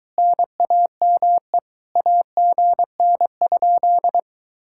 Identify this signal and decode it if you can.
Morse code, heard as a keyed tone, about 23 wpm, 700 Hz. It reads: NAME AGN?